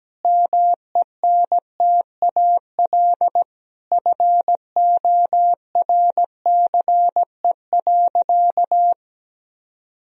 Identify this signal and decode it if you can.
Morse code, heard as a keyed tone, about 17 wpm, 700 Hz. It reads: MENTAL FORCE.
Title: Sentences from Top 1000 Words - Review Part 4 - 17wpm